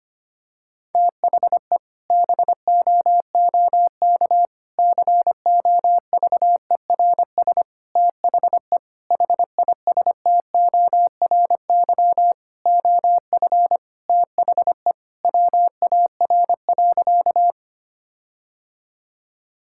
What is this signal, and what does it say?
Morse code, heard as a keyed tone, about 25 wpm, 700 Hz. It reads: THE BOOK COVERS THE HISTORY OF THE WAR.
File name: practice / 25wpm / Sentences with 4-Word Phrases - 25wpm